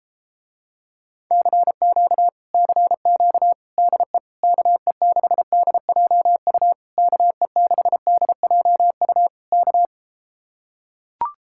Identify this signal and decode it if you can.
Morse code, heard as a keyed tone, about 33 wpm, 700 Hz. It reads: CQ CQ DE KE6DJU KE6DJU K